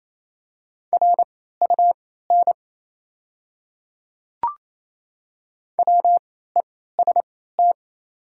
Morse code, heard as a keyed tone, about 28 wpm, spaced as for 16 wpm, 700 Hz. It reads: RUN WEST
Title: 3x Character Spacing - Top 500 Words - 28wpm